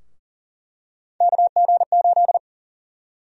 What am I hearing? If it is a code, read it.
Morse code, heard as a keyed tone, about 40 wpm, 700 Hz. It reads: KG8
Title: USA Call Sign Prefixes - 40wpm